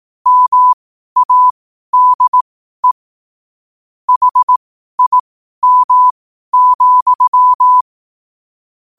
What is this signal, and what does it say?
Morse code, heard as a keyed tone, about 18 wpm, spaced as for 12 wpm, 1000 Hz. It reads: MADE HIM,
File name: qotdfiles / 12wpm